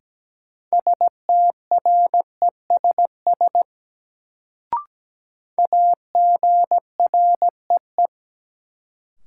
Morse code, heard as a keyed tone, about 17 wpm, 700 Hz. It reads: STRESS AGREE
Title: Sentences from Top 1000 Words - Review Part 6 - New Word Drill - 17wpm